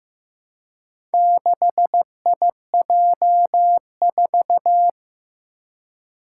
Morse code, heard as a keyed tone, about 15 wpm, 700 Hz. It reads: 6IJ4
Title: Four Challenging Letter-Number Combinations 15wpm